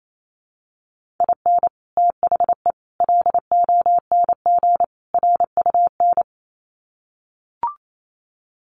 Morse code, heard as a keyed tone, about 28 wpm, 700 Hz. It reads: IN THE LONG RUN
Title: Sets of 4 Words 28wpm